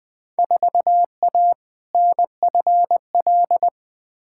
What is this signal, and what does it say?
Morse code, heard as a keyed tone, about 20 wpm, 700 Hz. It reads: 4A NFL
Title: Field Day Exchanges - 20wpm